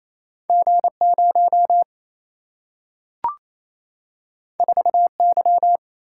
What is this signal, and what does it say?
Morse code, heard as a keyed tone, about 28 wpm, 700 Hz. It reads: G 0 4 Y
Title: Single-Letter Single-Number 28wpm